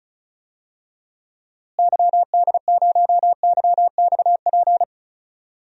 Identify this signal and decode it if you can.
Morse code, heard as a keyed tone, about 34 wpm, 700 Hz. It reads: YD0YXP
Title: DXCC-100 Call Signs - 6 Characters - Repeated 2x - 35wpm